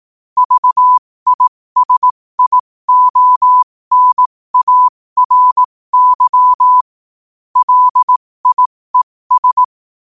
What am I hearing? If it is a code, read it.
Morse code, heard as a keyed tone, about 18 wpm, spaced as for 15 wpm, 1000 Hz. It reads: VISIONARY LIES
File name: qotdfiles / 15wpm